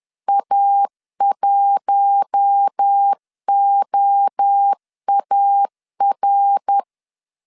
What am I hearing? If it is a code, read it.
Morse code, heard as a keyed tone, about 10 wpm, 800 Hz. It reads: A1OAR